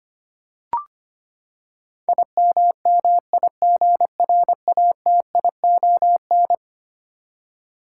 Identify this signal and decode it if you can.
Morse code, heard as a keyed tone, about 25 wpm, 700 Hz. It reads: IMMIGRATION
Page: morse-code-ninja-practice-25wpm